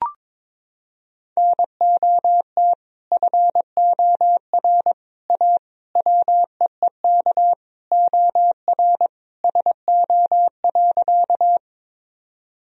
Morse code, heard as a keyed tone, about 22 wpm, 700 Hz. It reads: NOT FOR A WEEK OR SO.
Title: Sentences from Top 1000 Words - Review Part 2 - 22wpm